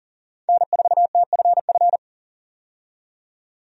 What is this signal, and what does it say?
Morse code, heard as a keyed tone, about 40 wpm, 700 Hz. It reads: N4TFF
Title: USA Call Signs - 5 Characters - 40wpm